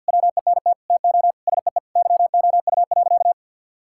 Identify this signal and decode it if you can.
Morse code, heard as a keyed tone, about 50 wpm, 700 Hz. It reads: WENT TO SEE YOU.
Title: Sentences from Top 1000 Words - Review Part 5 - 50wpm